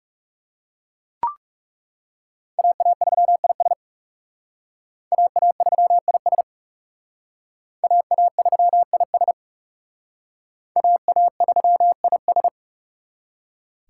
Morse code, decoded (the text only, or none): AA3IS AA3IS AA3IS AA3IS